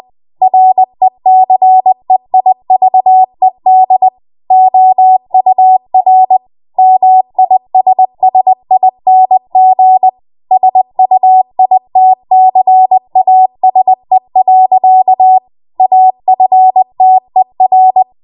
Morse code, heard as a keyed tone, about 20 wpm, 750 Hz. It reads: RECEIVED OUR MISSING SUITCASE. AFTER
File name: Morse / 20WPM